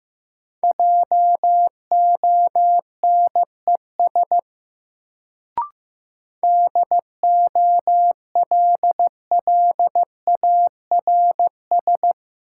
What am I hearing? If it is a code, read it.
Morse code, heard as a keyed tone, about 15 wpm, 700 Hz. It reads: JONES DOLLARS
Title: Sets of 5 Words - New Word Drill - 15wpm